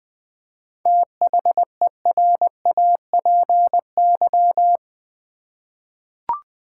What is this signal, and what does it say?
Morse code, heard as a keyed tone, about 20 wpm, 700 Hz. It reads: THERAPY
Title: 7 Letter Words - First 3 Letters Given - 20wpm